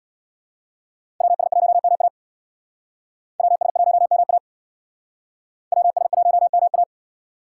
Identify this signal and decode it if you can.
Morse code, heard as a keyed tone, about 62 wpm, 700 Hz. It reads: WS1MA WS1MA WS1MA
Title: Speed Racing - USA Call Signs 50wpm